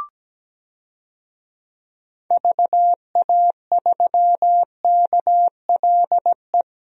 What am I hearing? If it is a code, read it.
Morse code, heard as a keyed tone, about 17 wpm, 700 Hz. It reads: VA3KLE
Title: DXCC-100 Call Signs - 6 Characters - Repeated 3x - 17wpm